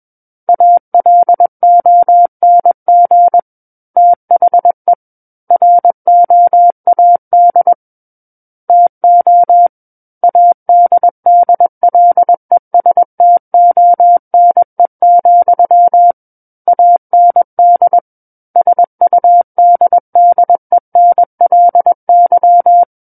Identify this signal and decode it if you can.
Morse code, heard as a keyed tone, about 21 wpm, 700 Hz. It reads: ALONG THE ROAD TO ADDLESTONE, AND SUDDENLY
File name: Book One / 12-Chapter 12 - 21 WPM